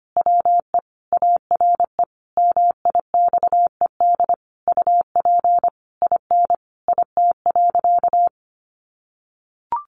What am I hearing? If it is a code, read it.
Morse code, heard as a keyed tone, about 25 wpm, 700 Hz. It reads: WE ARE MIXED UP IN IT.
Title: Sentences from Top 1500 Words 25wpm